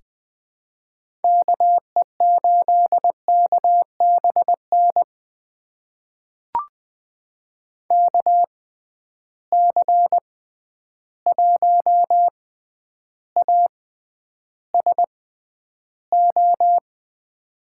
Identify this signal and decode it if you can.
Morse code, heard as a keyed tone, about 20 wpm, spaced as for 10 wpm, 700 Hz. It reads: KE8KBN KC1ASO
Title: USA Call Signs - 6x Character Spacing - 20wpm